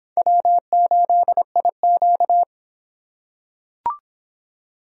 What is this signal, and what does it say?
Morse code, heard as a keyed tone, about 23 wpm, 700 Hz. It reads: W8IQ